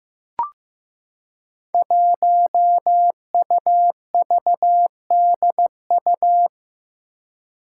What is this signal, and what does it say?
Morse code, heard as a keyed tone, about 15 wpm, 700 Hz. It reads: 1UVDU
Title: Five Challenging Letter-Number Combinations 15wpm